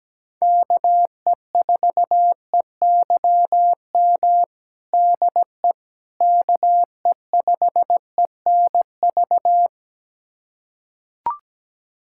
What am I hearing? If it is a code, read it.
Morse code, heard as a keyed tone, about 17 wpm, 700 Hz. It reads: KE4EYM DE KE5ENV
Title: USA Callsign DE Callsign - 6 Characters - 17wpm